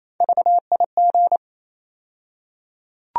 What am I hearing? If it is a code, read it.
Morse code, heard as a keyed tone, about 28 wpm, 700 Hz. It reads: VIG
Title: USA Call Sign Postfixes - 3 Letters 28wpm